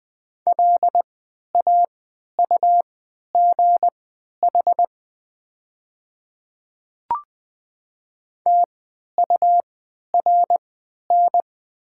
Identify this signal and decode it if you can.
Morse code, heard as a keyed tone, about 20 wpm, spaced as for 11 wpm, 700 Hz. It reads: LAUGH TURN
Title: 3x Character Spacing - Top 500 Words - 20wpm